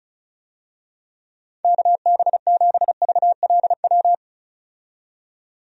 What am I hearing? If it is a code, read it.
Morse code, heard as a keyed tone, about 34 wpm, 700 Hz. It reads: KB7VLW
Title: USA Call Signs - 6 Characters - 35wpm